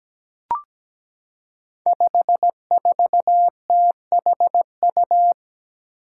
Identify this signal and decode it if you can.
Morse code, heard as a keyed tone, about 17 wpm, 700 Hz. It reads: 54THU
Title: Five Challenging Letter-Number Combinations 17wpm